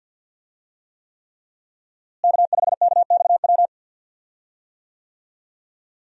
Morse code, heard as a keyed tone, about 52 wpm, 700 Hz. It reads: K5KXW
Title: USA Call Signs - 5 Characters - Repeated 3x - 50wpm